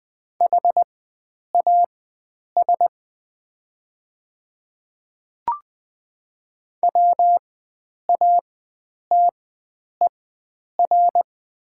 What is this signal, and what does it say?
Morse code, heard as a keyed tone, about 20 wpm, spaced as for 9 wpm, 700 Hz. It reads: HAS WATER
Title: 4x Character Spacing - Top 100 Words - 20wpm